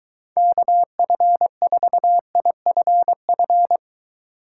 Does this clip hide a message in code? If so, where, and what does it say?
Yes: Morse code from 0.4 to 3.8 s, heard as a keyed tone, about 23 wpm, 700 Hz. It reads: KF4IFF